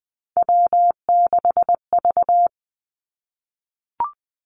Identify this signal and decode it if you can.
Morse code, heard as a keyed tone, about 20 wpm, 700 Hz. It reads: W6V
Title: Three Challenging Letter-Number Combinations 20wpm